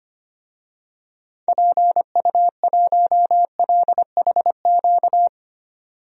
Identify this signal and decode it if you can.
Morse code, heard as a keyed tone, about 25 wpm, 700 Hz. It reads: PU1LHQ
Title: DXCC-100 Call Signs - 6 Characters - 25wpm